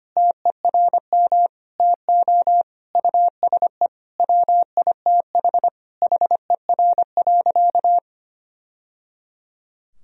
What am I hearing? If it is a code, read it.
Morse code, heard as a keyed tone, about 25 wpm, 700 Hz. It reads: TERM TO USE WITH HER.
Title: Sentences from Top 1000 Words - Review Part 4 - 25wpm